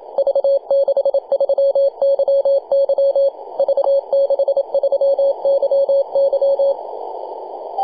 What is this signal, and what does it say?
Morse code, heard as a keyed tone, about 27 wpm, 550 Hz. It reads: V63YY V63YY
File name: posts / 2015-nov-21-0614z-v63yy-24-905mhz-cw